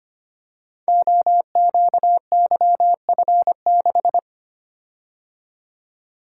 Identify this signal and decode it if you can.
Morse code, heard as a keyed tone, about 25 wpm, 700 Hz. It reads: OQYF6
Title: Code Groups - 5 Letter:Numbers - 25wpm